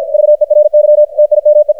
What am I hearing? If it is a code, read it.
Morse code, heard as a keyed tone, about 52 wpm, 600 Hz, with radio noise. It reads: VERY TIME
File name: archived / CRSnet